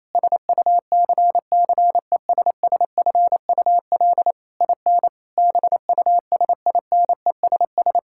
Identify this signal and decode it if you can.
Morse code, heard as a keyed tone, about 28 wpm, 700 Hz. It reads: SUCCESSFUL IN BUSINESS
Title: Sentences from Top 1000 Words - Review Part 6 - 28wpm